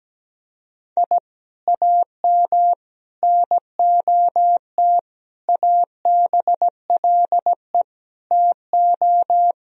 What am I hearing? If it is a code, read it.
Morse code, heard as a keyed tone, about 17 wpm, 700 Hz. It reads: I AM NOT ABLE TO
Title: Sentences from Top 1000 Words - Review Part 2 - 17wpm